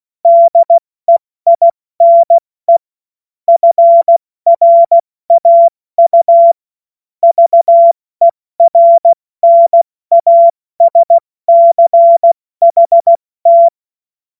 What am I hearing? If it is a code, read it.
Morse code, heard as a keyed tone, about 16 wpm, 680 Hz. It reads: DEINE FRAU VERNASCHT
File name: Morsekurs / Witze / Witz 29 April 2025 70 F80